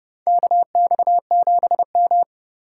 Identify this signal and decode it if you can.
Morse code, heard as a keyed tone, about 30 wpm, 700 Hz. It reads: KX7M
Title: K1USN SST QSO Practice - 30wpm